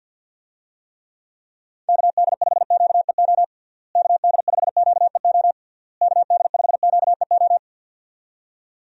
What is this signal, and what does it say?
Morse code, heard as a keyed tone, about 48 wpm, 700 Hz. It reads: KD5QEO KD5QEO KD5QEO
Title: USA Call Signs - 6 Characters - Repeated 3x - 50wpm